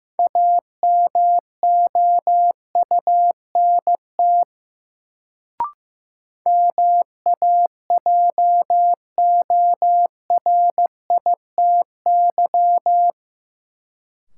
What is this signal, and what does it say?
Morse code, heard as a keyed tone, about 15 wpm, 700 Hz. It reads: AMOUNT MAJORITY
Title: Sets of 3 Words - New Word Drill - 15wpm